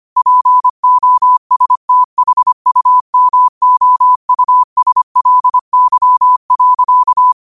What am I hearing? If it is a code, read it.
Morse code, heard as a keyed tone, about 25 wpm, 1000 Hz. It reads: POSTHUMOUSLY.
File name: qotdfiles / 25wpm